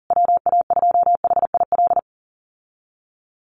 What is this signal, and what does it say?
Morse code, heard as a keyed tone, about 40 wpm, 700 Hz. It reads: WA2HIL
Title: USA Call Signs Repeated 3x 40wpm